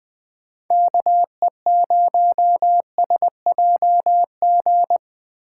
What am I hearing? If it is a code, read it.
Morse code, heard as a keyed tone, about 19 wpm, 700 Hz. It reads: KE0SJG